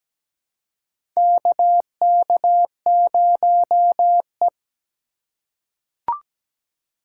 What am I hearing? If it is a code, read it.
Morse code, heard as a keyed tone, about 17 wpm, 700 Hz. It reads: KK0E
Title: USA Call Signs - 4 Characters - 2x Character Spacing - 17wpm